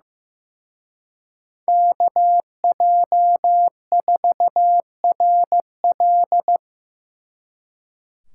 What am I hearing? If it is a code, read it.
Morse code, heard as a keyed tone, about 18 wpm, 700 Hz. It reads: KJ4RL
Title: Speed Racing - USA 2x2 Call Signs 15wpm